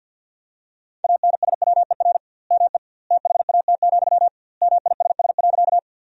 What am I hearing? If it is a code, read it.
Morse code, heard as a keyed tone, about 52 wpm, 700 Hz. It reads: ANSWER ME THAT, MISS.
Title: 2x Word-Spacing - Sentences from Top 500 Words - 50wpm